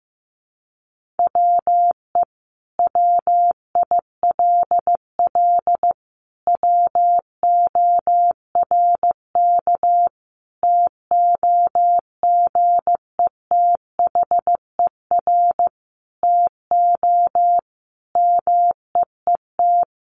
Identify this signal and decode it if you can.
Morse code, heard as a keyed tone, about 15 wpm, 700 Hz. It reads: WE WILL WORK TOGETHER TO MEET
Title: Sentences from Top 1100 Words 15wpm